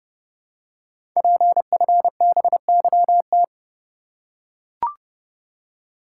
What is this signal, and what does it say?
Morse code, heard as a keyed tone, about 30 wpm, 700 Hz. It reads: PFBYT
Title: Code Groups - 5 Letters - 30wpm